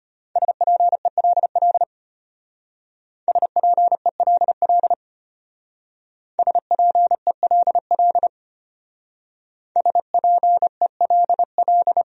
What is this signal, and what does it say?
Morse code, heard as a keyed tone, about 32 wpm, 700 Hz. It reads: SPELL SPELL SPELL SPELL